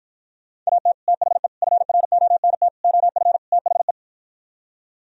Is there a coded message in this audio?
Yes: Morse code from 0.7 to 3.9 s, heard as a keyed tone, about 52 wpm, 700 Hz. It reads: AT THE FRONT OF THE